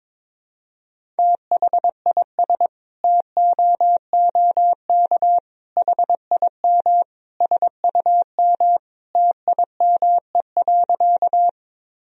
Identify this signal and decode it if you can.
Morse code, heard as a keyed tone, about 21 wpm, 700 Hz. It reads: THIS TOOK HIM SUM TIME.
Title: Sentences from Top 300 Words and Abbreviations 22wpm